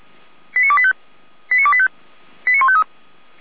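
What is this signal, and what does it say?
Five-tone selective call (ZVEI), heard as quick runs of five tones, tones of about 70 ms. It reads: 89285 89285 89153
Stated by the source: ZVEI five-tone selective call received on SDR, from the Feuerwehr Orlamünde fire brigade server.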